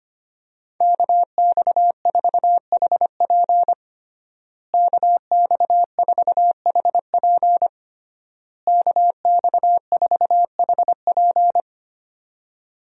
Morse code, heard as a keyed tone, about 25 wpm, 700 Hz. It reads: KX4HP KX4HP KX4HP
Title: USA Call Signs - 5 Characters - Repeated 3x - 25wpm